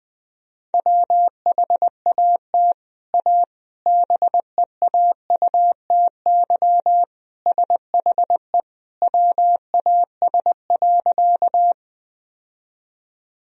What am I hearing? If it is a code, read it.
Morse code, heard as a keyed tone, about 20 wpm, 700 Hz. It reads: WHAT A BEAUTY SHE WAS.
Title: Sentences from Top 1000 Words - Review Part 2 - 20wpm